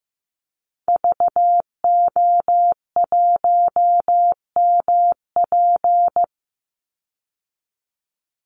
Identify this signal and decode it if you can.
Morse code, heard as a keyed tone, about 15 wpm, 700 Hz. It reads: VO1MP
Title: Call Signs Repeated 3x - 15wpm